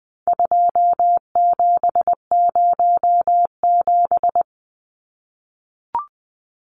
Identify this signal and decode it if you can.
Morse code, heard as a keyed tone, about 20 wpm, 700 Hz. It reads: 2707
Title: Numbers - 4-Digit 20wpm